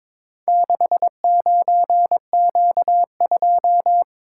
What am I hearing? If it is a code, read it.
Morse code, heard as a keyed tone, about 22 wpm, 700 Hz. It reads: 69Q2
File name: practice / 22wpm / Four Letter-Number Combinations - 22wpm